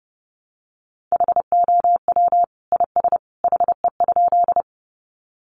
Morse code, heard as a keyed tone, about 29 wpm, 700 Hz. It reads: HOW IS HE?
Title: Sentences from Top 100 Words 30wpm